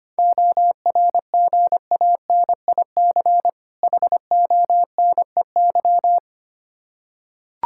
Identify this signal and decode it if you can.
Morse code, heard as a keyed tone, about 25 wpm, 700 Hz. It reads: ORGANIC HONEY
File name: practice / 25wpm / Difficult Words with Hints - 25wpm